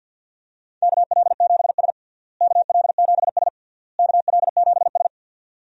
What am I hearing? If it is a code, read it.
Morse code, heard as a keyed tone, about 52 wpm, 700 Hz. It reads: KL7S KL7S KL7S
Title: DXCC-100 Call Signs - 4 Characters - Repeated 3x - 50wpm